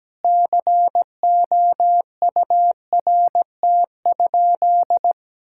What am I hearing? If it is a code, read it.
Morse code, heard as a keyed tone, about 17 wpm, 700 Hz. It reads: COURT?
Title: Sentences from Top 1600 Words 17wpm